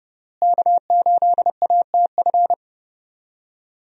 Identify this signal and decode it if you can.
Morse code, heard as a keyed tone, about 30 wpm, 700 Hz. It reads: K8ATF
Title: USA Call Signs - 5 Characters - Repeated 2x - 30wpm